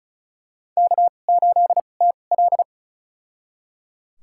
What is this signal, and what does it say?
Morse code, heard as a keyed tone, about 36 wpm, spaced as for 25 wpm, 700 Hz. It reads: K8TL
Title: USA Call Signs - 4 Characters - 2x Character Spacing - 35wpm